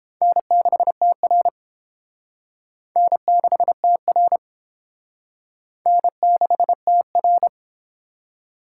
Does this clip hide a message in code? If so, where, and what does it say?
Yes: Morse code from 0.2 to 7.5 s, heard as a keyed tone, about 29 wpm, 700 Hz. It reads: N6TR N6TR N6TR